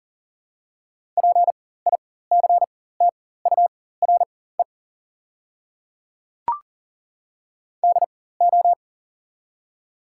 Morse code, heard as a keyed tone, about 40 wpm, spaced as for 19 wpm, 700 Hz. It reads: PICTURE DO
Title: 4x Character Spacing - Top 200 Words - 40wpm